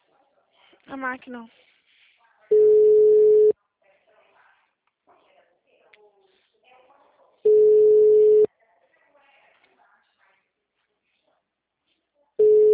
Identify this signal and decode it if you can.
Ringback tone (Europe, 425 Hz), with speech.